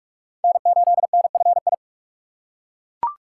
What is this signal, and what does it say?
Morse code, heard as a keyed tone, about 42 wpm, 700 Hz. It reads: N8NFI
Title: USA Call Signs - 5 Characters - 3x Character Spacing - 45wpm